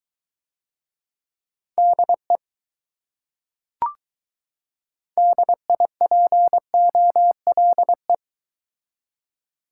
Morse code, heard as a keyed tone, about 20 wpm, 700 Hz. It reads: DE DIPOLE